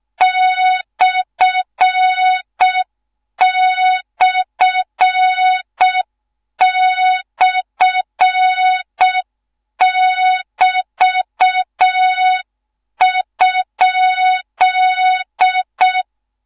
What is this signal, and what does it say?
Morse code, heard as a keyed tone, about 6 wpm, 3700 Hz, with noise. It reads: ///<BT>?